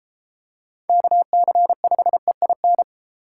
Morse code, heard as a keyed tone, about 31 wpm, 700 Hz. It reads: KC5EIN